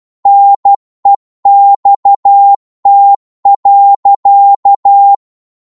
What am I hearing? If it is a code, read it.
Morse code, heard as a keyed tone, about 12 wpm, 800 Hz. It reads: NEXT.
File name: cwmp3 / psg-1-12wpm